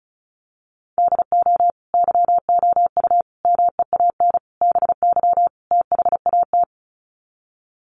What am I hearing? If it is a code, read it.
Morse code, heard as a keyed tone, about 34 wpm, 700 Hz. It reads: DO YOU MEAN BY THAT
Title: Sets of 5 Words Encore2 35wpm